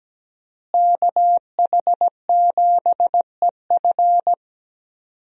Morse code, heard as a keyed tone, about 17 wpm, 700 Hz. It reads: KH7EF
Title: USA Call Signs - 5 Characters - Repeated 2x - 17wpm